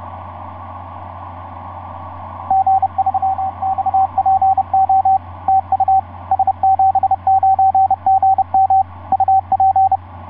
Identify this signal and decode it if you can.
Morse code, heard as a keyed tone, about 30 wpm, 770 Hz, with radio noise. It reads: G3XPO TU S79GM UP
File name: s79gm f6eyb etc 12m cw 24mar2010 1207